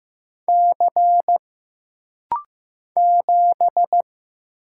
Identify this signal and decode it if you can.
Morse code, heard as a keyed tone, about 15 wpm, 700 Hz. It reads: C 7
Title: Single Letter-Number Alt - Rapid-Fire 15wpm